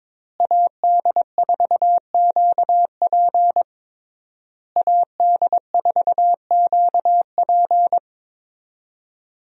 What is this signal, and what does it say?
Morse code, heard as a keyed tone, about 22 wpm, 700 Hz. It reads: AD4QP AD4QP
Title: USA Call Signs - 5 Characters - Repeated 3x - 22wpm